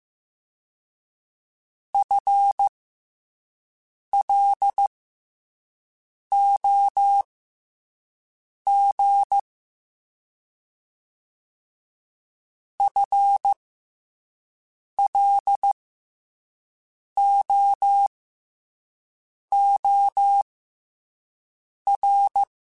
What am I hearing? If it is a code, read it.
Morse code, heard as a keyed tone, about 15 wpm, spaced as for 5 wpm, 800 Hz. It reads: FLOG FLOOR
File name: FL Runs / FL Runs mp3s / FL Runs @5